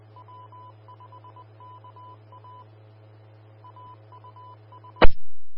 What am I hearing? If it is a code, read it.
Morse code, heard as a keyed tone, about 20 wpm, 1000 Hz. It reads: W5KA AUS
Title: Austin, TX Amateur Repeaters